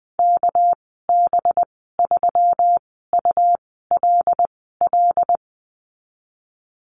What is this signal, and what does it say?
Morse code, heard as a keyed tone, about 20 wpm, 700 Hz. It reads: K B 3 U L L
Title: 238 Lesson 38g - Letter q - Call Signs 20wpm